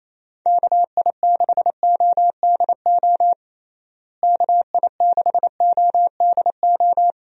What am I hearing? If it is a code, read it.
Morse code, heard as a keyed tone, about 28 wpm, 700 Hz. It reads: KI6ODO KI6ODO
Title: USA Call Signs - 6 Characters - Repeated 3x - 28wpm